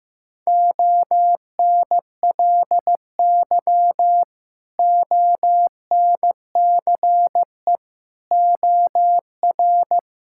Morse code, heard as a keyed tone, about 15 wpm, 700 Hz. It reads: ONLY ONCE OR
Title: Sentences from Top 1600 Words 15wpm